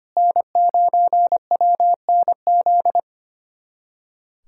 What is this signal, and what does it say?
Morse code, heard as a keyed tone, about 25 wpm, 700 Hz. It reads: N9WNZ
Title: USA Callsign DE Callsign - 5 Characters - 25wpm